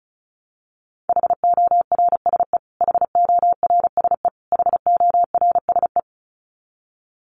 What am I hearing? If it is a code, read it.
Morse code, heard as a keyed tone, about 34 wpm, 700 Hz. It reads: HORSE HORSE HORSE
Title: Top 300 Words Repeated 35wpm